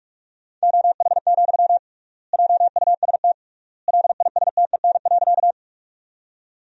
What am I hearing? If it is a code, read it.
Morse code, heard as a keyed tone, about 46 wpm, 700 Hz. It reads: OH, JUST LISTEN.
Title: 3x Word-Spacing - Sentences from Top 500 Words - 45wpm